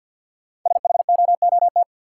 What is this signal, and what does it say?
Morse code, heard as a keyed tone, about 51 wpm, 700 Hz. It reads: SHOOT
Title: Sentences from Top 1200 Words - New Word Drill - 50wpm